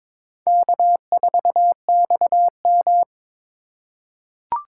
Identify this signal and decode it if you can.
Morse code, heard as a keyed tone, about 22 wpm, 700 Hz. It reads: K4XM
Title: USA Call Signs - 4 Characters - 4x Character Spacing - 22wpm